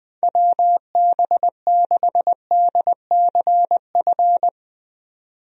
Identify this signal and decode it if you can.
Morse code, heard as a keyed tone, about 21 wpm, 700 Hz. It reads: WB6DCF
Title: USA Call Signs - 6 Characters - 20wpm